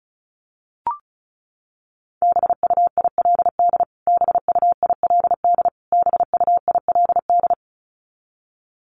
Morse code, heard as a keyed tone, about 34 wpm, 700 Hz. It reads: BUILD BUILD BUILD
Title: Top 200 Words Repeated 35wpm